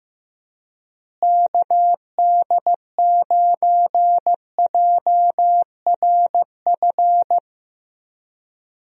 Morse code, heard as a keyed tone, about 15 wpm, 700 Hz. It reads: KD9JRF
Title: USA Call Signs - 6 Characters - Repeated 3x - 15wpm